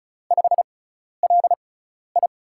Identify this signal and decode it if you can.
Morse code, heard as a keyed tone, about 36 wpm, spaced as for 12 wpm, 700 Hz. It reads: 5LI
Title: USA Call Signs - 6x Character Spacing - 35wpm